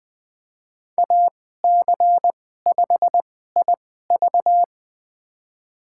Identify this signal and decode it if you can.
Morse code, heard as a keyed tone, about 20 wpm, spaced as for 14 wpm, 700 Hz. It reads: AC5IV
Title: USA Call Signs - 5 Characters - 2x Character Spacing - 20wpm